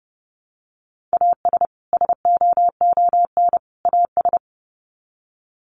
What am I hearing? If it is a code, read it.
Morse code, heard as a keyed tone, about 30 wpm, 700 Hz. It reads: AS SOON AS
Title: Sets of 3 Words 30wpm